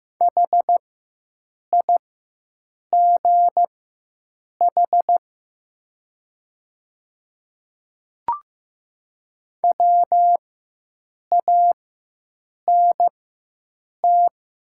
Morse code, heard as a keyed tone, about 15 wpm, spaced as for 7 wpm, 700 Hz. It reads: HIGH WANT
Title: 4x Character Spacing - Top 200 Words - 15wpm